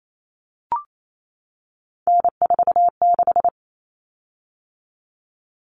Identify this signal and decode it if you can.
Morse code, heard as a keyed tone, about 28 wpm, 700 Hz. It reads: N46
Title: Common Cut Numbers - 3-Digit 28wpm